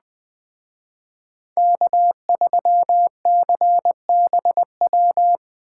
Morse code, heard as a keyed tone, about 20 wpm, 700 Hz. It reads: K3CBW